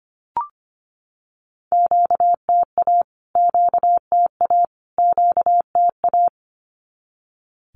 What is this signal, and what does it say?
Morse code, heard as a keyed tone, about 25 wpm, 700 Hz. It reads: QTA QTA QTA
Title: Q-Codes Repeated 3x 25wpm